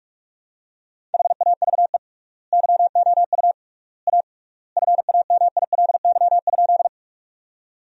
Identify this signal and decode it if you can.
Morse code, heard as a keyed tone, about 46 wpm, 700 Hz. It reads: HAVE YOU A FAMILY?
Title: 3x Word-Spacing - Sentences from Top 400 Words - 45wpm